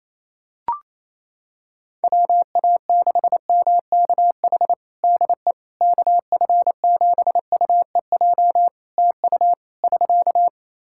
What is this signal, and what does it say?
Morse code, heard as a keyed tone, about 28 wpm, 700 Hz. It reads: WA6MKH DE KF7UEJ TU <SK>
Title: K1USN SST QSO Practice 28wpm